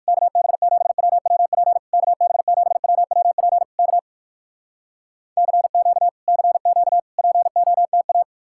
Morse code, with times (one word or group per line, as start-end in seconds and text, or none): 0.1-1.8 KB7WWP
1.9-3.6 KB7WWP
3.8-4.0 K
5.4-6.1 CQ
6.3-7.0 CQ
7.2-8.2 POTA